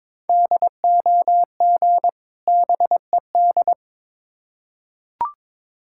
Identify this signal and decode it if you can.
Morse code, heard as a keyed tone, about 22 wpm, 700 Hz. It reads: DOG BED
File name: Difficult Words with Hints - 22wpm